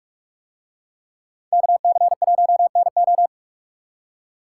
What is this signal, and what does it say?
Morse code, heard as a keyed tone, about 44 wpm, 700 Hz. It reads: KC1NO